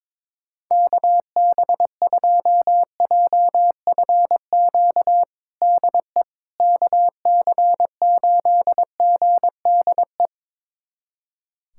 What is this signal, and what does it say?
Morse code, heard as a keyed tone, about 22 wpm, 700 Hz. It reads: KB2JFQ DE KC8GDE
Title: USA Callsign DE Callsign - 6 Characters - 22wpm